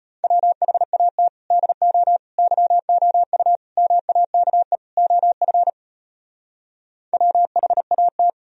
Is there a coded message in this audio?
Yes: Morse code from 0.2 to 8.3 s, heard as a keyed tone, about 33 wpm, 700 Hz. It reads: WHAT DO YOU MAKE OF WHAT